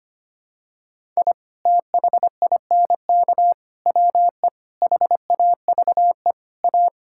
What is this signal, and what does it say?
Morse code, heard as a keyed tone, about 25 wpm, 700 Hz. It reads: I THINK WE HAVE A
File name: practice / 25wpm / Sentences with 4-Word Phrases - 25wpm